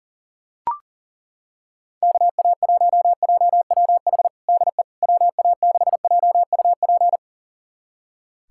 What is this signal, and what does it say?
Morse code, heard as a keyed tone, about 40 wpm, 700 Hz. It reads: KA1JWH DE WA6JUP
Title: USA Callsign DE Callsign - 6 Characters - 40wpm